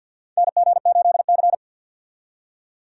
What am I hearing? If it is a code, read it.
Morse code, heard as a keyed tone, about 49 wpm, 700 Hz. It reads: NG8C
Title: USA Call Signs - 3x Character Spacing - 50wpm